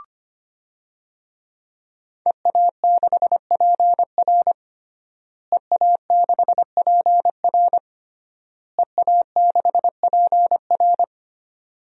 Morse code, heard as a keyed tone, about 25 wpm, 700 Hz. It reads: EA6PR EA6PR EA6PR
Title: DXCC-100 Call Signs - 5 Characters - Repeated 3x - 25wpm